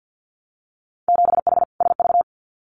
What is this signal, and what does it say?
Morse code, heard as a keyed tone, about 50 wpm, 700 Hz. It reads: 7H SV
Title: Winter Field Day Exchanges Encore 50wpm